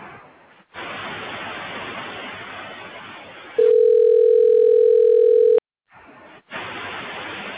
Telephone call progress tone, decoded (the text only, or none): ringback tone